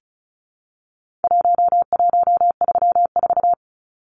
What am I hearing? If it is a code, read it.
Morse code, heard as a keyed tone, about 34 wpm, 700 Hz. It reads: 1134
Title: Numbers - Contest Style - 4-Digit 35wpm